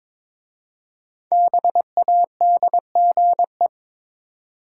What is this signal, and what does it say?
Morse code, heard as a keyed tone, about 22 wpm, 700 Hz. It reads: BADGE
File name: 5 Letter Words - First 3 Letters Given - 22wpm